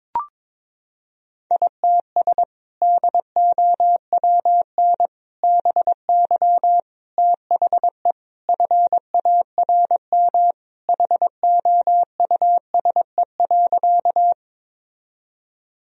Morse code, heard as a keyed tone, about 22 wpm, 700 Hz. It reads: ITS DOWN BY THE FARM HOUSE.
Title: Sentences from Top 1000 Words - Review Part 2 - 22wpm